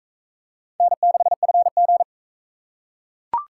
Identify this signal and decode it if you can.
Morse code, heard as a keyed tone, about 41 wpm, 700 Hz. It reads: N6FG